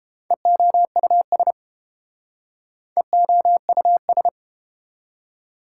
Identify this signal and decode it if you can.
Morse code, heard as a keyed tone, about 31 wpm, 700 Hz. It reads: EOUS EOUS